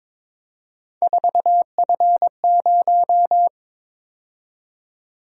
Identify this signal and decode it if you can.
Morse code, heard as a keyed tone, about 22 wpm, 700 Hz. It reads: 4F0
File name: practice / 22wpm / Three Letter-Number Combinations - 22wpm